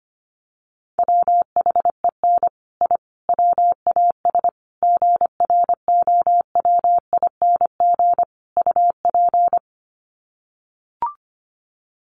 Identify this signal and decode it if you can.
Morse code, heard as a keyed tone, about 25 wpm, 700 Hz. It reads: WHEN I WAS GROWING UP
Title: Sets of 5 Words Encore 25wpm